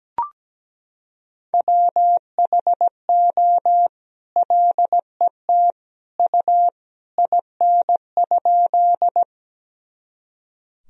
Morse code, heard as a keyed tone, about 17 wpm, 700 Hz. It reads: WHO LET U IN?
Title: Sentences from Top 300 Words and Abbreviations 17wpm